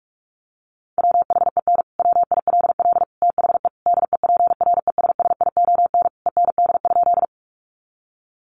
Morse code, heard as a keyed tone, about 44 wpm, 700 Hz. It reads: WHEN WILL THE DEPRESSION END?
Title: Sentences from Top 1300 Words 45wpm